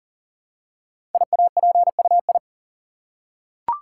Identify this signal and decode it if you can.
Morse code, heard as a keyed tone, about 38 wpm, 690 Hz. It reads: IAPUI
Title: Code Groups - 5 Letters - 40wpm